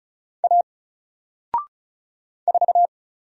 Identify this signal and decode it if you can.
Morse code, heard as a keyed tone, about 36 wpm, 700 Hz. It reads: A 4
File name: Single-Letter-Number-35wpm